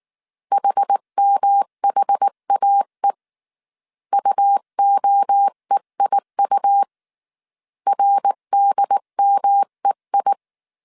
Morse code, heard as a keyed tone, about 19 wpm, 800 Hz. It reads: HMHAE UOEIU RDMEI
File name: CW / DET / 1 3 1500C S B 18wpm 1916